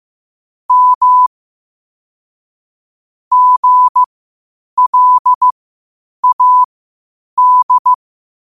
Morse code, heard as a keyed tone, about 15 wpm, spaced as for 8 wpm, 1000 Hz. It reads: M GLAD
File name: qotdfiles / 8wpm